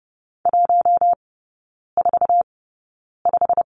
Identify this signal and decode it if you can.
Morse code, heard as a keyed tone, about 30 wpm, spaced as for 9 wpm, 700 Hz. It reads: 145
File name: course / 30:10wpm / 087 Lesson 15e - Number 5 - Three Characters 30wpm